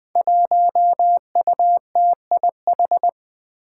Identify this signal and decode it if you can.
Morse code, heard as a keyed tone, about 20 wpm, 700 Hz. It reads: 1UTIH
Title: Five Challenging Letter-Number Combinations 20wpm